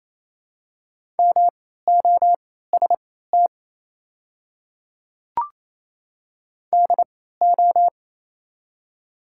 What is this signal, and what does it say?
Morse code, heard as a keyed tone, about 28 wpm, spaced as for 16 wpm, 700 Hz. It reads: MOST DO
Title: 3x Character Spacing - Top 100 Words - 28wpm